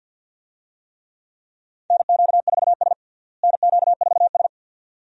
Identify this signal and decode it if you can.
Morse code, heard as a keyed tone, about 52 wpm, 700 Hz. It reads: NQ4S NQ4S
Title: USA Call Signs - 4 Characters - Repeated 2x - 50wpm